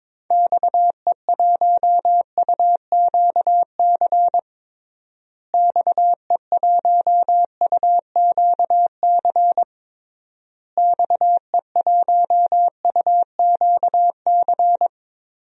Morse code, heard as a keyed tone, about 22 wpm, 700 Hz. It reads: XE1UQC XE1UQC XE1UQC
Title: DXCC-100 Call Signs - 6 Characters - Repeated 3x - 22wpm